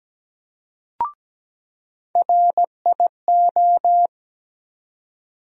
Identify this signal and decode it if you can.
Morse code, heard as a keyed tone, about 17 wpm, 700 Hz. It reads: RIO
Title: USA Call Sign Postfixes - 3 Letters 17wpm